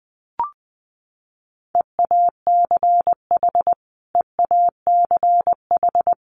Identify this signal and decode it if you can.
Morse code, heard as a keyed tone, about 20 wpm, 700 Hz. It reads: EACH EACH
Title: Top 300 Words Repeated 20wpm